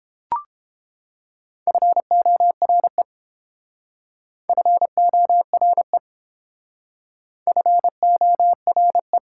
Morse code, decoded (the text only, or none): FORE FORE FORE